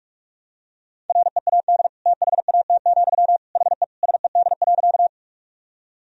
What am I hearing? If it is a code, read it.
Morse code, heard as a keyed tone, about 46 wpm, 700 Hz. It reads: READ THAT, HE SED.
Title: Sentences from Top 200 Words and Abbreviations 45wpm